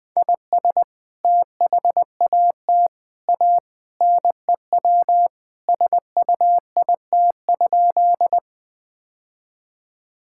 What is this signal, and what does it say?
Morse code, heard as a keyed tone, about 20 wpm, 700 Hz. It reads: IS THAT A NEW SUIT?
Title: Sentences from Top 1600 Words 20wpm